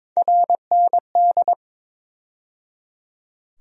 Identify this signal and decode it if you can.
Morse code, heard as a keyed tone, about 22 wpm, 700 Hz. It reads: RND
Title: USA Call Sign Postfixes - 3 Letters 22wpm